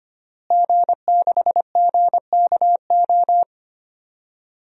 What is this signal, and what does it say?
Morse code, heard as a keyed tone, about 25 wpm, 700 Hz. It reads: G6GKO
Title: DXCC-100 Call Signs - 5 Characters - Repeated 3x - 25wpm